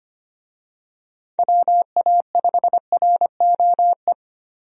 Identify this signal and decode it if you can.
Morse code, heard as a keyed tone, about 25 wpm, 700 Hz. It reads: WA5ROE
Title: USA Call Signs - 6 Characters - Repeated 3x - 25wpm